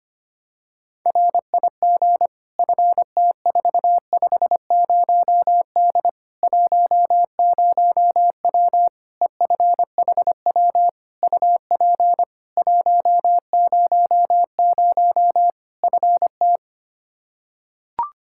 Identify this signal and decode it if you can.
Morse code, heard as a keyed tone, about 25 wpm, 700 Hz. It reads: RIG FT450D 10W EFHW UP 100 FT
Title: K1USN SST QSO Practice 25wpm